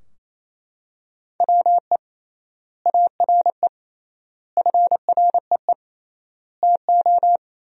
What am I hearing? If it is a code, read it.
Morse code, heard as a keyed tone, about 28 wpm, 700 Hz. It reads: WE ARE FREE TO
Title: 3x Word-Spacing - Sentences from Top 500 Words - 28wpm